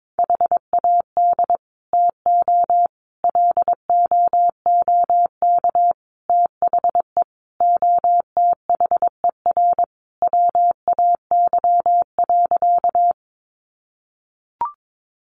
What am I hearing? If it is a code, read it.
Morse code, heard as a keyed tone, about 22 wpm, 700 Hz. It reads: HAD TO LOOK THE OTHER WAY.
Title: Sentences from Top 100 Words 22wpm